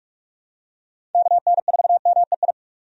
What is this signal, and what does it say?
Morse code, heard as a keyed tone, about 46 wpm, 700 Hz. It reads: KN4MEI
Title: USA Call Signs - 6 Characters - 45wpm